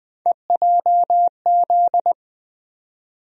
Morse code, heard as a keyed tone, about 20 wpm, 700 Hz. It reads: EJZ
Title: USA Call Sign Postfixes - 3 Letters 20wpm